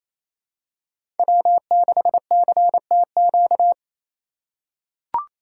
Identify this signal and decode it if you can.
Morse code, heard as a keyed tone, about 28 wpm, 700 Hz. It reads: W6CTQ
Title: USA Call Signs - 5 Characters - 4x Character Spacing - 28wpm